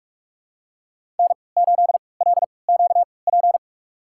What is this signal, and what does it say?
Morse code, heard as a keyed tone, about 46 wpm, spaced as for 26 wpm, 700 Hz. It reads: N8LQP
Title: USA Call Signs - 5 Characters - 3x Character Spacing - 45wpm